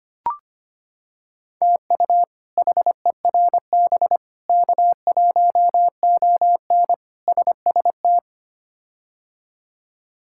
Morse code, heard as a keyed tone, about 25 wpm, 700 Hz. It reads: TU HERB K1ON SST
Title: K1USN SST QSO Practice - 25wpm